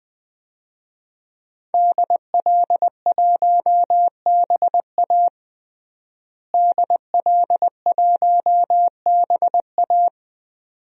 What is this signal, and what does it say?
Morse code, heard as a keyed tone, about 20 wpm, 700 Hz. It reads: DL1BA DL1BA
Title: DXCC-100 Call Signs - 5 Characters - Repeated 2x - 20wpm